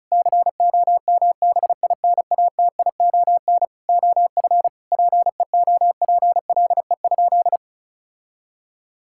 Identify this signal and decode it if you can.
Morse code, heard as a keyed tone, about 36 wpm, 700 Hz. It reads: COMBINATION OF PEOPLE?
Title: Sentences from Top 1000 Words - Review Part 6 - 35wpm